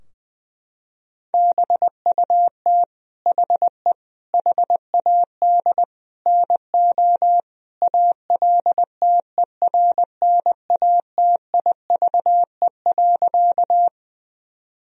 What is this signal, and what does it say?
Morse code, heard as a keyed tone, about 20 wpm, 700 Hz. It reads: BUT HE HAD NO ALTERNATIVE.
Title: Sentences from Top 1000 Words - Review Part 4 - 20wpm